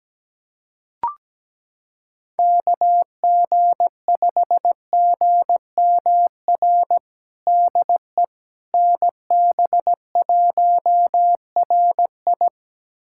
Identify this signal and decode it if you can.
Morse code, heard as a keyed tone, about 17 wpm, 700 Hz. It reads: KG5GMR DE NB1RI
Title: USA Callsign DE Callsign 17wpm